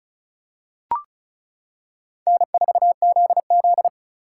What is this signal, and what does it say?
Morse code, heard as a keyed tone, about 36 wpm, 700 Hz. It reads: N4ZZ
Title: K1USN SST QSO Practice - 35wpm